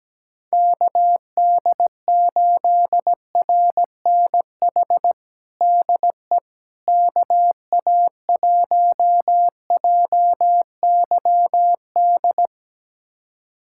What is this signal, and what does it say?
Morse code, heard as a keyed tone, about 17 wpm, 700 Hz. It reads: KD8RNH DE KA1JYD
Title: USA Callsign DE Callsign 17wpm